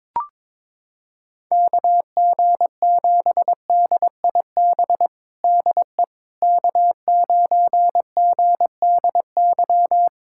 Morse code, heard as a keyed tone, about 22 wpm, 700 Hz. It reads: KG7DIB DE K9GDY
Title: USA Callsign DE Callsign 22wpm